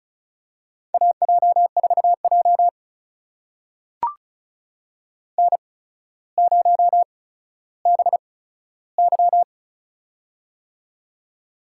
Morse code, heard as a keyed tone, about 36 wpm, spaced as for 15 wpm, 700 Hz. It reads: AJ4J N0BY
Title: USA Call Signs - 4 Characters - 8x Character Spacing - 35wpm